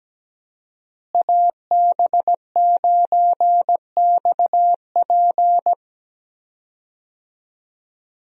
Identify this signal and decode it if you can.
Morse code, heard as a keyed tone, about 17 wpm, 700 Hz. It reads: AB9XP
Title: USA Call Signs - 5 Characters - Repeated 2x - 17wpm